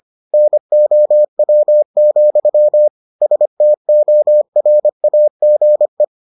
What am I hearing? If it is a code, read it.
Morse code, heard as a keyed tone, about 25 wpm, 600 Hz. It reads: NOW, STORAGE